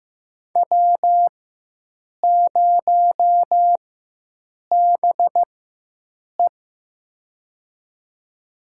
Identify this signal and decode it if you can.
Morse code, heard as a keyed tone, about 15 wpm, spaced as for 7 wpm, 700 Hz. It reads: W0BE
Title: USA Call Signs - 4 Characters - 4x Character Spacing - 15wpm